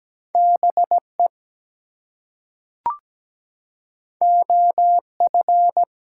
Morse code, heard as a keyed tone, about 17 wpm, 700 Hz. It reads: BE OF
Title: Two Letter Words - 17wpm